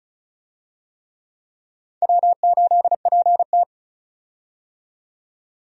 Morse code, heard as a keyed tone, about 36 wpm, 700 Hz. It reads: W8PT
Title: USA Call Signs - 4 Characters - 35wpm